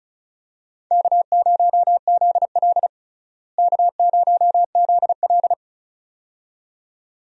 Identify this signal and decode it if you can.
Morse code, heard as a keyed tone, about 34 wpm, 700 Hz. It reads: K0ZL K0ZL
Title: USA Call Signs - 4 Characters - Repeated 2x - 35wpm